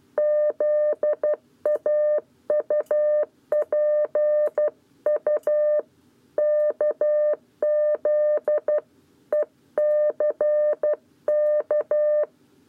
Morse code, heard as a keyed tone, about 12 wpm, 600 Hz. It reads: ZAUPU KZ ECK
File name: 7b lezione cw 1